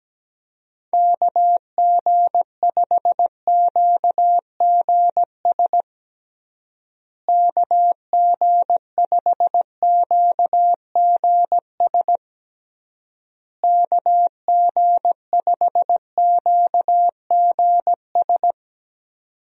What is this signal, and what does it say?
Morse code, heard as a keyed tone, about 17 wpm, 700 Hz. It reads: KG5QGS KG5QGS KG5QGS
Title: USA Call Signs - 6 Characters - Repeated 3x - 17wpm